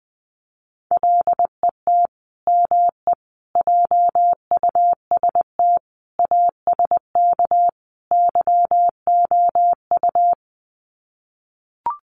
Morse code, heard as a keyed tone, about 20 wpm, 700 Hz. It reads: LET ME JUST ASK YOU
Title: Sets of 5 Words Encore2 20wpm